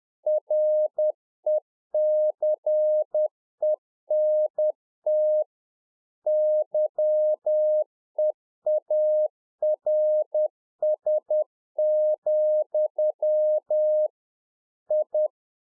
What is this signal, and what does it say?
Morse code, heard as a keyed tone, about 10 wpm, 610 Hz. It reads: RECENT YEARS, I